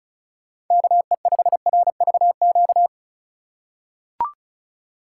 Morse code, heard as a keyed tone, about 36 wpm, 700 Hz. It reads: KE5RVQ
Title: USA Call Signs - 6 Characters - 4x Character Spacing - 35wpm